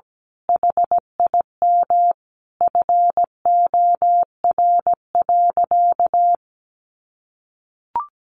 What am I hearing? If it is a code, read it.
Morse code, heard as a keyed tone, about 17 wpm, 700 Hz. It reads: HIM FOR.
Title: Sentences from Top 1000 Words 17wpm